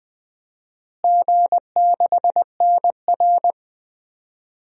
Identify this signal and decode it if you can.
Morse code, heard as a keyed tone, about 20 wpm, 700 Hz. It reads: G6NR
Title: DXCC-100 Call Signs - 4 Characters - Repeated 3x - 20wpm